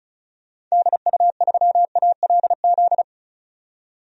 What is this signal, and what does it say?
Morse code, heard as a keyed tone, about 36 wpm, 700 Hz. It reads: DU3ALZ
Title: DXCC-100 Call Signs - 6 Characters - 35wpm